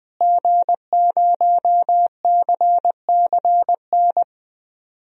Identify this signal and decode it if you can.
Morse code, heard as a keyed tone, about 20 wpm, 700 Hz. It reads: G0CCN
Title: DXCC-100 Call Signs - 5 Characters - Repeated 3x - 20wpm